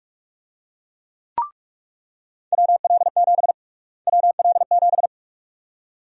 Morse code, heard as a keyed tone, about 46 wpm, 700 Hz. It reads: WL7 WL7
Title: USA Call Sign Prefixes - Repeated 2x - 45wpm